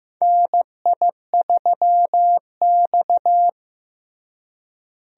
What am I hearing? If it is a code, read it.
Morse code, heard as a keyed tone, about 15 wpm, 700 Hz. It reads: NI3X
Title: USA Call Signs - 4 Characters - Repeated 2x - 15wpm